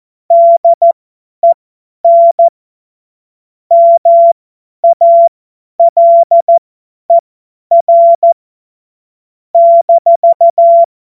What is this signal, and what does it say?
Morse code, heard as a keyed tone, about 14 wpm, spaced as for 10 wpm, 680 Hz. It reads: DEN MALER -
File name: Morsekurs / Witze / Witz 23 Januar 2024 50 F70